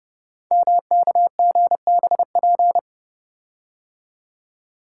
Morse code, heard as a keyed tone, about 30 wpm, 700 Hz. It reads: MKGBP
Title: Code Groups - 5 Letters - 30wpm